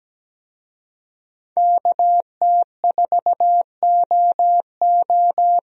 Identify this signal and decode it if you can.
Morse code, heard as a keyed tone, about 17 wpm, 700 Hz. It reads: KT4OO
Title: USA Call Signs - 5 Characters - 17wpm